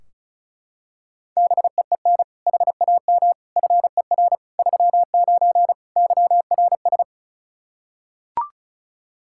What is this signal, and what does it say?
Morse code, heard as a keyed tone, about 34 wpm, 700 Hz. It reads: BEEN HAM FER 39 YRS
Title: K1USN SST QSO Practice 35wpm